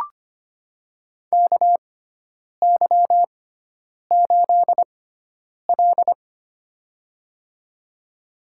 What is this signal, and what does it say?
Morse code, heard as a keyed tone, about 25 wpm, spaced as for 9 wpm, 700 Hz. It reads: KY8L
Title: USA Call Signs - 4 Characters - 6x Character Spacing - 25wpm